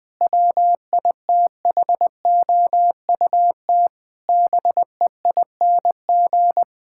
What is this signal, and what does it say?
Morse code, heard as a keyed tone, about 21 wpm, 700 Hz. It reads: WITHOUT BEING
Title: Sentences from Top 1000 Words - Review Part 6 - 20wpm